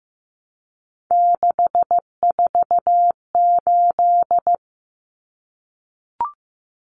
Morse code, heard as a keyed tone, about 15 wpm, 700 Hz. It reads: 648
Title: Numbers - 3-Digit 15wpm